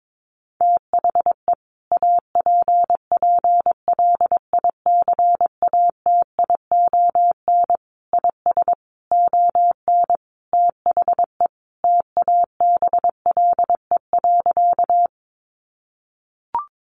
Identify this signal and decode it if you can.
Morse code, heard as a keyed tone, about 22 wpm, 700 Hz. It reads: THE APPLICATION IS ON THE TABLE.
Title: Sentences from Top 1200 Words 22wpm